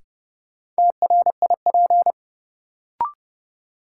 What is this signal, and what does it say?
Morse code, heard as a keyed tone, about 32 wpm, 700 Hz. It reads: TRIP
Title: Sentences from Top 1000 Words - Review Part 4 - New Word Drill - 30wpm